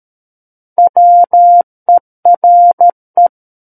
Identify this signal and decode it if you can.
Morse code, heard as a keyed tone, about 13 wpm, 700 Hz. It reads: WERE
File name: Book One / 04-Chapter 4 - 13 WPM